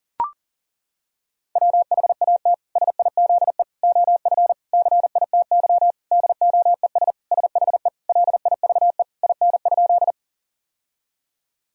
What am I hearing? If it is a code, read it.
Morse code, heard as a keyed tone, about 40 wpm, 700 Hz. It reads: WHAT SIZE OF CITY DOES SHE LIVE IN?
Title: Sentences from Top 1000 Words - Review Part 2 - 40wpm